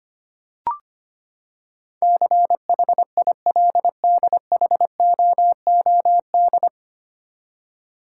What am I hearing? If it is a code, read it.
Morse code, heard as a keyed tone, about 25 wpm, 700 Hz. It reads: CHILDHOOD
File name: Nine Letter Words - 25wpm